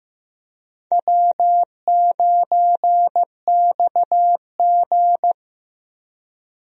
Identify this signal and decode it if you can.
Morse code, heard as a keyed tone, about 15 wpm, 700 Hz. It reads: W9XG
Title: USA Call Signs - 4 Characters - Repeated 2x - 15wpm